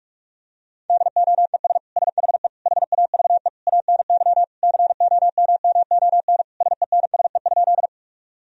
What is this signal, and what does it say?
Morse code, heard as a keyed tone, about 46 wpm, 700 Hz. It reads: DOES SHE HAVE ANY COMMON SENSE?
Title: Sentences from Top 1000 Words - Review Part 4 - 45wpm